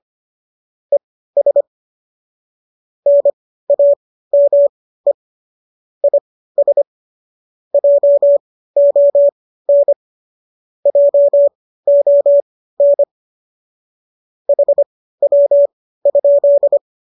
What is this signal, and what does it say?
Morse code, heard as a keyed tone, about 25 wpm, spaced as for 15 wpm, 570 Hz. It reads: ES NAME IS JON JON HW?
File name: qso207 15